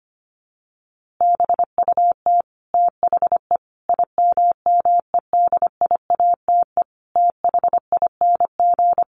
Morse code, heard as a keyed tone, about 25 wpm, 700 Hz. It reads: BUT THE IMMEDIATE THING
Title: A Sound Of Thunder 25wpm